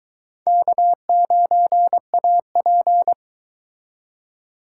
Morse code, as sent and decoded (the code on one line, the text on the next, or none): -.- ----. .- .--.
K9AP